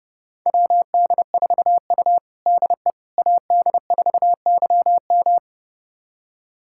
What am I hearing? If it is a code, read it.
Morse code, heard as a keyed tone, about 30 wpm, 700 Hz. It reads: WD4U DE AD4YM
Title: USA Callsign DE Callsign 30wpm